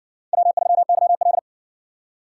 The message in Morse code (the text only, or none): W3YL